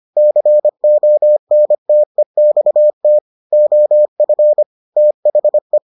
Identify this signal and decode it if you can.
Morse code, heard as a keyed tone, about 25 wpm, 600 Hz. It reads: CONTEXT OF THE